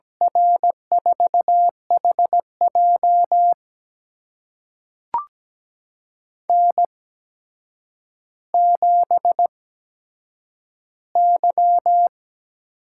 Morse code, heard as a keyed tone, about 17 wpm, spaced as for 8 wpm, 700 Hz. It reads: R4HJ N7Y
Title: USA Call Signs - 5 Characters - 8x Character Spacing - 17wpm